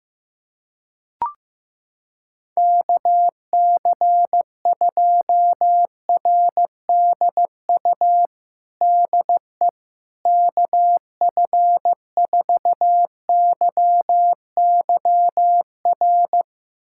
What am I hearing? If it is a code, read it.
Morse code, heard as a keyed tone, about 15 wpm, 700 Hz. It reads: KC2RDU DE KF4YYR
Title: USA Callsign DE Callsign 15wpm